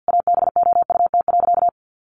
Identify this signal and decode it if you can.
Morse code, heard as a keyed tone, about 50 wpm, 700 Hz. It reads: ABOUT.
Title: Sentences from Top 1300 Words 50wpm